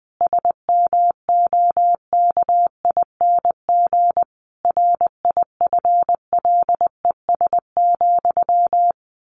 Morse code, heard as a keyed tone, about 20 wpm, 700 Hz. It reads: SMOKING RIFLES,
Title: A Sound Of Thunder 20wpm